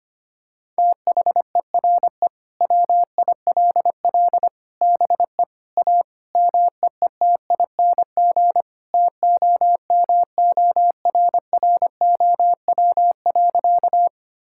Morse code, heard as a keyed tone, about 25 wpm, 700 Hz. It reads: THERE WILL BE A MEETING TOMORROW.
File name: practice / 25wpm / Sentences with 4-Word Phrases - 25wpm